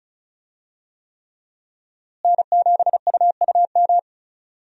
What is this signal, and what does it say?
Morse code, heard as a keyed tone, about 34 wpm, 700 Hz. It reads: N7UUM
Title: USA Call Signs - 5 Characters - Repeated 3x - 35wpm